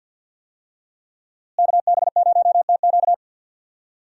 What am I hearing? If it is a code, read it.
Morse code, heard as a keyed tone, about 51 wpm, 700 Hz. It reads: KB0TQ
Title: USA Call Signs - 5 Characters - Repeated 3x - 50wpm